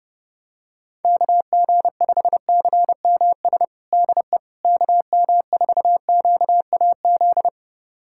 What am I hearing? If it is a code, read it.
Morse code, heard as a keyed tone, about 30 wpm, 700 Hz. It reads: KG5CMS DE KM4QAZ
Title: USA Callsign DE Callsign - 6 Characters - 30wpm